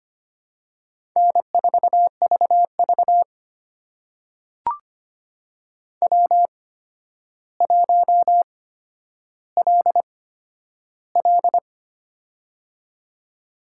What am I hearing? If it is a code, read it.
Morse code, heard as a keyed tone, about 25 wpm, spaced as for 11 wpm, 700 Hz. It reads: N4VV W1LL
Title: USA Call Signs - 4 Characters - 8x Character Spacing - 25wpm